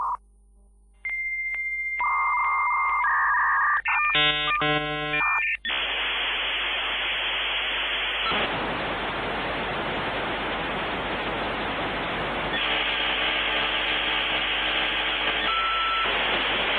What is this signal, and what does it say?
Dial-up modem handshake.
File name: glitch-fax-machine